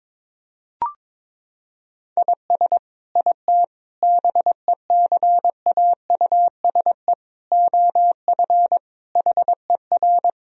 Morse code, heard as a keyed tone, about 22 wpm, 700 Hz. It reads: IS IT BECAUSE OF HER
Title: Sentences from Top 1000 Words - Review Part 3 - 22wpm